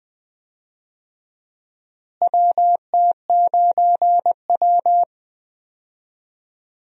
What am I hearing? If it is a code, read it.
Morse code, heard as a keyed tone, about 20 wpm, 700 Hz. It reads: WT9W
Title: USA Call Signs - 4 Characters - Repeated 3x - 20wpm